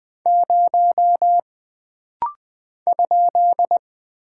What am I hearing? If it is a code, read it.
Morse code, heard as a keyed tone, about 20 wpm, 700 Hz. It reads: 0 ?
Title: Single Character (Letter:Number:Punctuation:Prosign) - Rapid-Fire - 20wpm